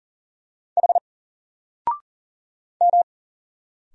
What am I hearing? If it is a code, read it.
Morse code, heard as a keyed tone, about 40 wpm, 700 Hz. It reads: H M
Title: Single Letter-Number 40wpm